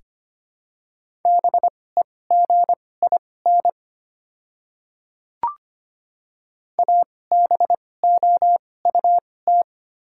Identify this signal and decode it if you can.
Morse code, heard as a keyed tone, about 25 wpm, spaced as for 18 wpm, 690 Hz. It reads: BEGIN ABOUT
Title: 2x Character Spacing - Top 300 Words - 25wpm